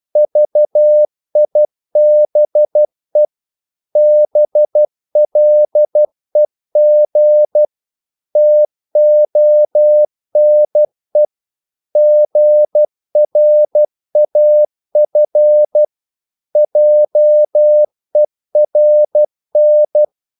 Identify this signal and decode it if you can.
Morse code, heard as a keyed tone, about 12 wpm, 600 Hz. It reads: VIBE BLEG TONE GRAF JERN
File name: Korte ord DK 12wpm